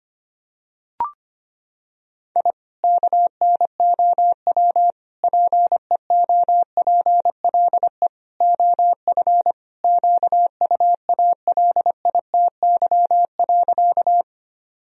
Morse code, heard as a keyed tone, about 25 wpm, 700 Hz. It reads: I KNOW PEOPLE OF QUALITY.
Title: Sentences from Top 1000 Words - Review Part 4 - 25wpm